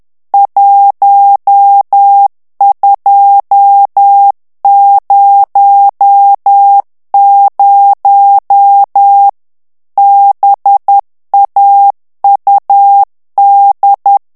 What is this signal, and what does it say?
Morse code, heard as a keyed tone, about 10 wpm, 800 Hz. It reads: 1200 BAUD
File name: ANGLAIS texte clair 3 vitesse 10 mots minute